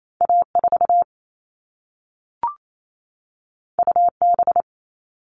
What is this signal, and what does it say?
Morse code, heard as a keyed tone, about 28 wpm, 700 Hz. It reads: A4 UB
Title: All Cut Numbers - 2-Digit 28wpm